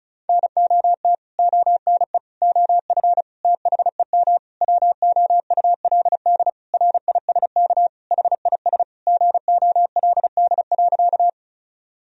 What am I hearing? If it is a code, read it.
Morse code, heard as a keyed tone, about 34 wpm, 700 Hz. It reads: NOT ONE OF THEM WOULD RISK HIS GOLD.
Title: Sentences from Top 1000 Words - Review Part 4 - 35wpm